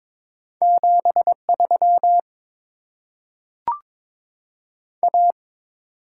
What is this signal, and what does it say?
Morse code, heard as a keyed tone, about 22 wpm, 700 Hz. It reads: 73 A
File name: Top 20 QSO Elements - 8x Character Spacing - 22wpm